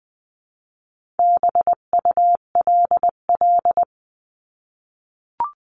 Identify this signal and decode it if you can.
Morse code, heard as a keyed tone, about 20 wpm, 700 Hz. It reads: BULL
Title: 259 Lesson 42d - Words 18-20wpm